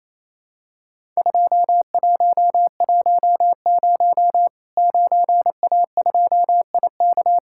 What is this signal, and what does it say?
Morse code, heard as a keyed tone, about 28 wpm, 700 Hz. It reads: 2110 9A2IK
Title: Worked All Europe DX Contest - QTC Exchange - 28wpm